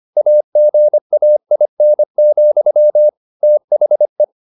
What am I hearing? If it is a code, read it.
Morse code, heard as a keyed tone, about 25 wpm, 600 Hz. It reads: AGAIN, THE